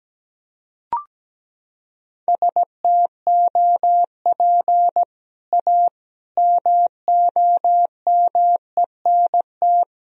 Morse code, heard as a keyed tone, about 17 wpm, 700 Hz. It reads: STOP A MOMENT
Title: Sentences from Top 1000 Words - Review Part 1 - 17wpm